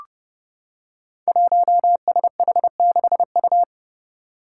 Morse code, heard as a keyed tone, about 30 wpm, 700 Hz. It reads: 1SH6U
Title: Five Challenging Letter-Number Combinations 30wpm